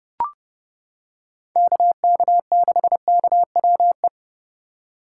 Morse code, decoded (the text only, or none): KK6KWE